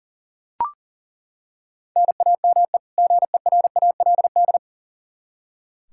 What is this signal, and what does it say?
Morse code, heard as a keyed tone, about 40 wpm, 700 Hz. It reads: NAME GERALD
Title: Practice QSO - Exchange 1 - Beginning - 40wpm